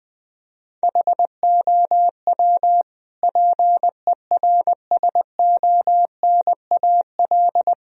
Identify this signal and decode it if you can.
Morse code, heard as a keyed tone, about 20 wpm, 700 Hz. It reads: HOW PERSONAL
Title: Sentences from Top 1000 Words - Review Part 4 - 20wpm